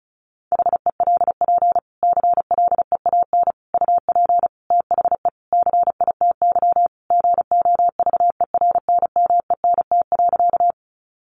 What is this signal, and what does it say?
Morse code, heard as a keyed tone, about 34 wpm, 700 Hz. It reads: HELP CLEAN UP THE CITY GOVERNMENT.
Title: Sentences from Top 1100 Words 35wpm